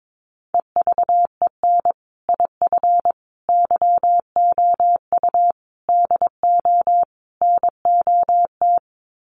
Morse code, heard as a keyed tone, about 22 wpm, 700 Hz. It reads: EVEN IF YOU DO NOT
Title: Sets of 5 Words Encore 22wpm